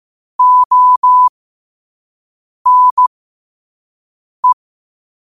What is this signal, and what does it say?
Morse code, heard as a keyed tone, about 15 wpm, spaced as for 5 wpm, 1000 Hz. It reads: ONE
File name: qotdfiles / 5wpm